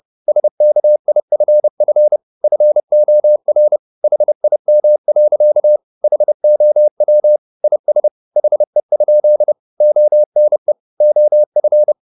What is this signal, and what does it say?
Morse code, heard as a keyed tone, about 30 wpm, 600 Hz. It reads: SKIFF FOR HIM. HOW IS HE? ONE OF